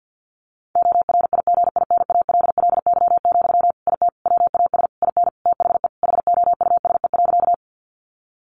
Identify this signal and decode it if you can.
Morse code, heard as a keyed tone, about 50 wpm, 700 Hz. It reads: ORIGINALLY, IT WAS IN THE HOUSE.
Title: Sentences from Top 1500 Words 50wpm